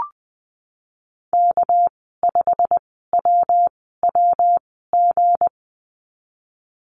Morse code, heard as a keyed tone, about 20 wpm, spaced as for 14 wpm, 700 Hz. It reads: K5WWG
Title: 219 Lesson 35g - Number 0 - Call Signs 20wpm